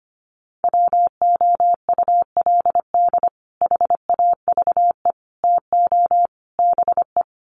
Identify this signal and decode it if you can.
Morse code, heard as a keyed tone, about 25 wpm, 700 Hz. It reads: WOULD HAVE TO BE
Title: Sets of 4 Words 25wpm